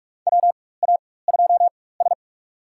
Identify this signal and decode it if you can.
Morse code, heard as a keyed tone, about 46 wpm, spaced as for 21 wpm, 700 Hz. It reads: WA2S
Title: USA Call Signs - 4 Characters - 4x Character Spacing - 45wpm